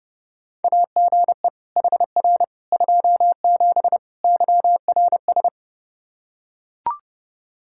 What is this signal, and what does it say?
Morse code, heard as a keyed tone, about 30 wpm, 700 Hz. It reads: AGE HR 27 YRS
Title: Practice QSO - Exchange 3 - More Info - 30wpm